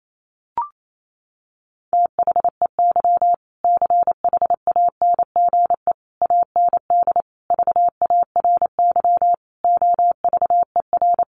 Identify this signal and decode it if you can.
Morse code, heard as a keyed tone, about 28 wpm, 700 Hz. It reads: THEY CHANGE AND VARY OVER
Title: Sentences from Top 1200 Words 28wpm